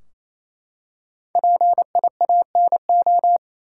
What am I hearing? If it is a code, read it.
Morse code, heard as a keyed tone, about 28 wpm, 700 Hz. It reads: PIANO
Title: Five Letter Words - 28wpm